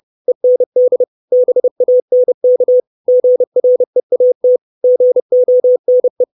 Morse code, heard as a keyed tone, about 30 wpm, 500 Hz. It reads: END BANK GREAT GONE
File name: 632 Words 30WPM